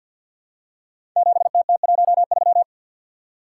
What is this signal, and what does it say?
Morse code, heard as a keyed tone, about 52 wpm, 700 Hz. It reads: 7TT13
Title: Code Groups - 5 Letter:Numbers - 50wpm